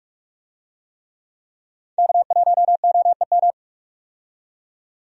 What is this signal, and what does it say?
Morse code, heard as a keyed tone, about 46 wpm, 700 Hz. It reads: K1OEM
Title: USA Call Signs - 5 Characters - 45wpm